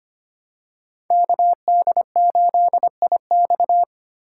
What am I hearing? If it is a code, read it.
Morse code, heard as a keyed tone, about 25 wpm, 700 Hz. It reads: KD8IX
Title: USA Call Signs - 4x Character Spacing - 25wpm